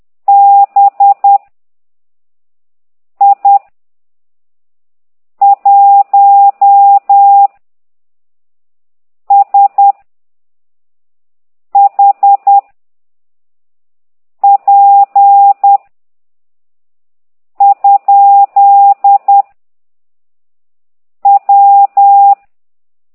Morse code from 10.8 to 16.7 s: H P